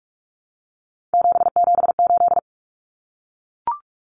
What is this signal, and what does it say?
Morse code, heard as a keyed tone, about 44 wpm, 700 Hz. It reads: 778
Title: Numbers - 3-Digit 45wpm